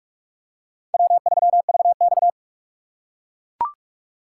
Morse code, heard as a keyed tone, about 42 wpm, 700 Hz. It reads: W3VX